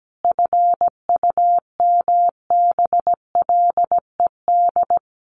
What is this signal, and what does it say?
Morse code, heard as a keyed tone, about 17 wpm, 700 Hz. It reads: FUMBLED
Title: A Sound Of Thunder 17wpm